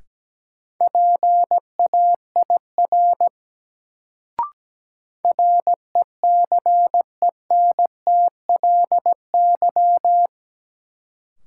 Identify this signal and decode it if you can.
Morse code, heard as a keyed tone, about 17 wpm, 700 Hz. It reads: PAIR RECENTLY
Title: Sentences from Top 1200 Words - New Word Drill - 17wpm